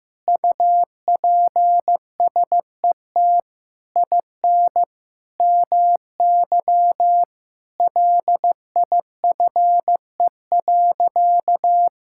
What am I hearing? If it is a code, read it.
Morse code, heard as a keyed tone, about 15 wpm, 700 Hz. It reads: UPSET IN MY LIFE.
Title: Sentences from Top 1000 Words - Review Part 5 - 15wpm